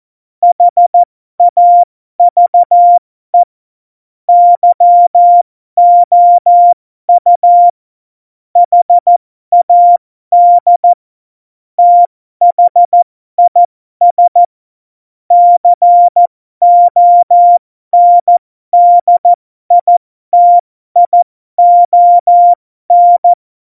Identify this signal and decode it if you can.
Morse code, heard as a keyed tone, about 14 wpm, spaced as for 12 wpm, 700 Hz. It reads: HAVE YOU HAD THIS CONDITION
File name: Morsekurs / Witze